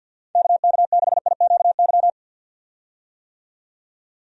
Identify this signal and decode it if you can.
Morse code, heard as a keyed tone, about 52 wpm, 700 Hz. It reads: KK6IQY
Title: USA Call Signs - 6 Characters - 50wpm